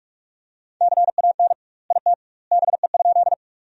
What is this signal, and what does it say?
Morse code, heard as a keyed tone, about 45 wpm, 700 Hz. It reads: CAN IT BE?
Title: 2x Word-Spacing - Sentences from Top 100 Words - 45wpm